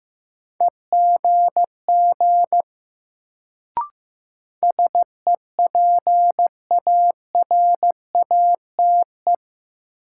Morse code, heard as a keyed tone, about 15 wpm, 700 Hz. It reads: EGG SEPARATE
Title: Sentences from Top 1000 Words - Review Part 7 - New Word Drill - 15wpm